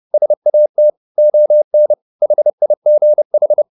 Morse code, heard as a keyed tone, about 30 wpm, 600 Hz. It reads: SAT ON HIGH